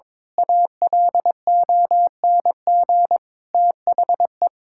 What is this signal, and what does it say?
Morse code, heard as a keyed tone, about 22 wpm, 700 Hz. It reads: ALONG THE
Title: Pangram Challenge v2 - 22wpm